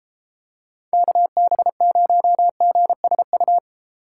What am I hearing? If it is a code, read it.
Morse code, heard as a keyed tone, about 34 wpm, 700 Hz. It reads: KB0GSU